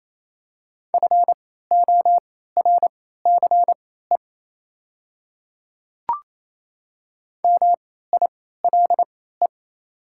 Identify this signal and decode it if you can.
Morse code, heard as a keyed tone, about 28 wpm, spaced as for 16 wpm, 700 Hz. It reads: FORCE MILE
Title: 3x Character Spacing - Top 500 Words - 28wpm